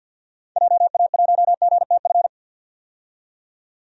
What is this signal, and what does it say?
Morse code, heard as a keyed tone, about 46 wpm, 700 Hz. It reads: JA1GTF
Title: DXCC-100 Call Signs - 6 Characters - 50wpm